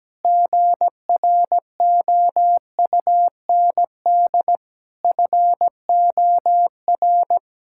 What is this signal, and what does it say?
Morse code, heard as a keyed tone, about 17 wpm, 700 Hz. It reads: GROUND FOR
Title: Sentences from Top 1600 Words 17wpm